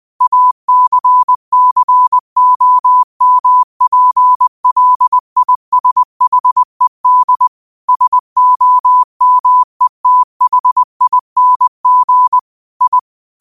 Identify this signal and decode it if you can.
Morse code, heard as a keyed tone, about 20 wpm, 1000 Hz. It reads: ACCOMPLISHED SOMETHING I
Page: New quotes every day in morse code at 20 Words per minute.